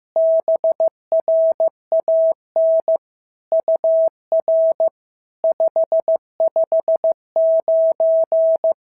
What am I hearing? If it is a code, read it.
Morse code, heard as a keyed tone, about 14 wpm, 650 Hz. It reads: BRAN UR 559
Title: SOTA QSO Practice 15wpm